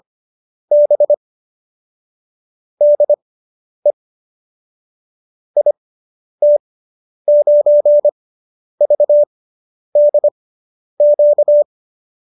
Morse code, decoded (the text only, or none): B DE IT9VDQ